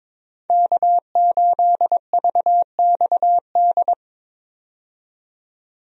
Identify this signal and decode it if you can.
Morse code, heard as a keyed tone, about 22 wpm, 700 Hz. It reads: K8VXD
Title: USA Call Signs - 5 Characters - 22wpm